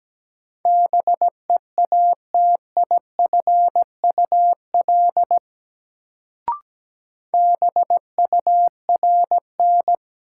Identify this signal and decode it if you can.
Morse code, heard as a keyed tone, about 17 wpm, 700 Hz. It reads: BEATIFUL BURN
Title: Sentences from Top 1300 Words - New Word Drill Alt - 17wpm